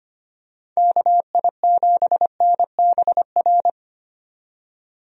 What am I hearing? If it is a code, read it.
Morse code, heard as a keyed tone, about 25 wpm, 700 Hz. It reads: KI7NBR
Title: USA Call Signs - 6 Characters - 25wpm